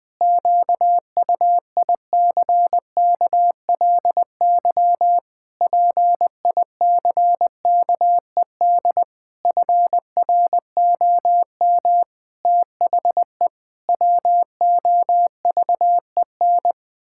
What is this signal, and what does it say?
Morse code, heard as a keyed tone, about 20 wpm, 700 Hz. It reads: QUICKLY PICKED FROM THE WOVEN
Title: Pangram Challenge v2 - 20wpm